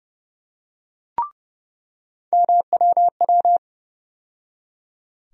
Morse code, heard as a keyed tone, about 30 wpm, 700 Hz. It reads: MWW
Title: USA Call Sign Postfixes - 3 Letters 30wpm